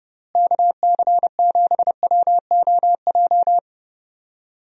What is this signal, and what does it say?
Morse code, heard as a keyed tone, about 30 wpm, 700 Hz. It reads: KC7WOJ
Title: USA Call Signs - 6 Characters - Repeated 3x - 30wpm